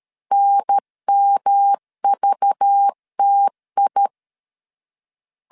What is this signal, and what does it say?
Morse code, heard as a keyed tone, about 13 wpm, 800 Hz. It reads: NMVTI
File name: LEZIONI / 7 GF9Z / 7 GF9 12wpm 2319